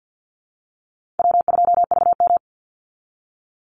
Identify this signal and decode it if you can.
Morse code, heard as a keyed tone, about 50 wpm, 700 Hz. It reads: W2VM
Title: Easy Call Signs - 50wpm